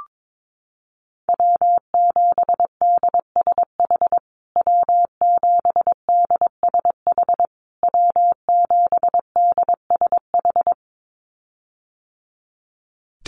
Morse code, heard as a keyed tone, about 22 wpm, 700 Hz. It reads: W7DSH W7DSH W7DSH
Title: USA Call Signs Repeated 3x 22wpm